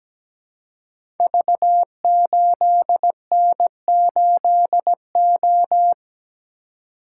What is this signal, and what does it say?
Morse code, heard as a keyed tone, about 17 wpm, 700 Hz. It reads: V8N8O
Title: Code Groups - 5 Letter:Numbers - 17wpm